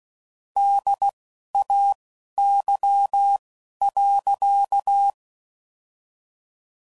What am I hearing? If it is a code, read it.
Morse code, heard as a keyed tone, about 16 wpm, spaced as for 12 wpm, 800 Hz. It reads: DAY.
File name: CW Code Groups @10WPM / Weather Sets / Weather 5 12